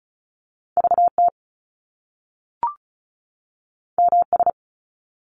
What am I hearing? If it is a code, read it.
Morse code, heard as a keyed tone, about 36 wpm, 700 Hz. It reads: VT MS
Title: US States 35wpm